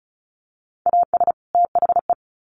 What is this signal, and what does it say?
Morse code, heard as a keyed tone, about 34 wpm, 700 Hz. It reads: AS THE
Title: Sets of 2 Words 35wpm